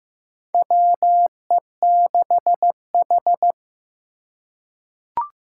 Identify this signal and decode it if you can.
Morse code, heard as a keyed tone, about 15 wpm, 700 Hz. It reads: WE6H
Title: USA Call Signs - 4 Characters - 6x Character Spacing - 15wpm